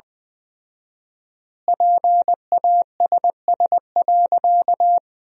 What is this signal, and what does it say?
Morse code, heard as a keyed tone, about 21 wpm, 700 Hz. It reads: PASS.
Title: 4x Word-Spacing - Sentences from Top 400 Words - 20wpm